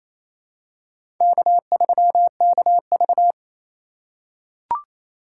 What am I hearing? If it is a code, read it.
Morse code, heard as a keyed tone, about 28 wpm, 700 Hz. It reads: K3KV
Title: USA Call Signs - 4 Characters - 8x Character Spacing - 28wpm